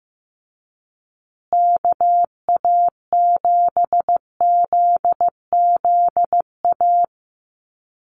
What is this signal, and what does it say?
Morse code, heard as a keyed tone, about 15 wpm, 700 Hz. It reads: KA7ZZA
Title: USA Call Signs Repeated 3x 15wpm